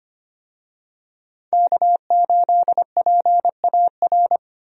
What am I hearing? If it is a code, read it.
Morse code, heard as a keyed tone, about 25 wpm, 700 Hz. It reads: K8PAR
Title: USA Call Signs - 5 Characters - 25wpm